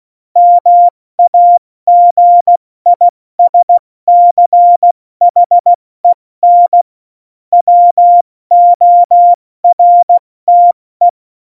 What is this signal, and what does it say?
Morse code, heard as a keyed tone, about 16 wpm, 700 Hz. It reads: MAGISCHEN WORTE
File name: Morsekurs / Witze / Witz 27. August 2019, 70 F80